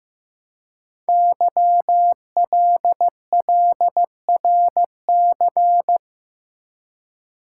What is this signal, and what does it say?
Morse code, heard as a keyed tone, about 15 wpm, 700 Hz. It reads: YLLRC
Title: Code Groups - 5 Letters - 15wpm